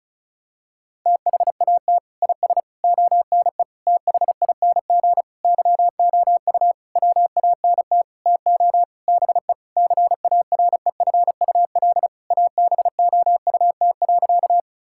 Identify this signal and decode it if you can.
Morse code, heard as a keyed tone, about 34 wpm, 700 Hz. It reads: THAT IS ONE THING YOU WANT TO BE CAREFUL ABOUT.
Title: Sentences from Top 1600 Words 35wpm